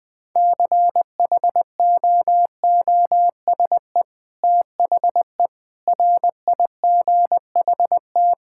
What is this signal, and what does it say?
Morse code, heard as a keyed tone, about 20 wpm, 700 Hz. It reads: CHOOSE THE RIGHT
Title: Sentences with 2-Word Phrases - Spoken Beginning - 20wpm